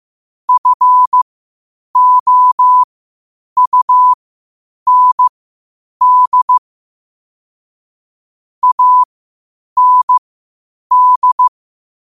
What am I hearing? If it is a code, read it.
Morse code, heard as a keyed tone, about 15 wpm, spaced as for 8 wpm, 1000 Hz. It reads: FOUND AND
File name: qotdfiles / 8wpm